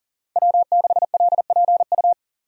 Morse code, heard as a keyed tone, about 40 wpm, 700 Hz. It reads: W6LPU
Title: USA Call Signs - 5 Characters - 40wpm